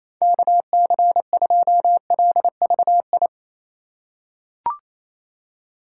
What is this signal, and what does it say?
Morse code, heard as a keyed tone, about 28 wpm, 700 Hz. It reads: KC2LVI
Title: USA Call Signs - 6 Characters - 3x Character Spacing - 28wpm